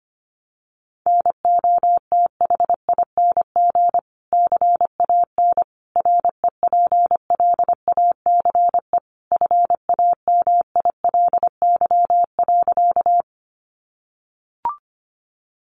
Morse code, heard as a keyed tone, about 25 wpm, 700 Hz. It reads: NOTHING CAN REPLACE FAMILY.
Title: Sentences from Top 1400 Words 25wpm